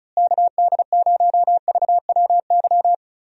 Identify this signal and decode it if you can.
Morse code, heard as a keyed tone, about 36 wpm, 700 Hz. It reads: KD0VWY
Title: USA Call Signs - 6 Characters - Repeated 2x - 35wpm